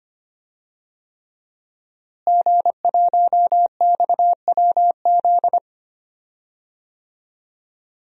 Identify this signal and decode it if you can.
Morse code, heard as a keyed tone, about 25 wpm, 700 Hz. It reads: G1XWZ
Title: DXCC-100 Call Signs - 5 Characters - Repeated 3x - 25wpm